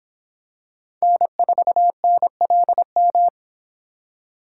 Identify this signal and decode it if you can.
Morse code, heard as a keyed tone, about 26 wpm, 700 Hz. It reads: N4NLM